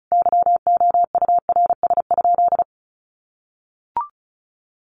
Morse code, heard as a keyed tone, about 34 wpm, 700 Hz. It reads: YOURS?
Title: Sentences from Top 1500 Words 35wpm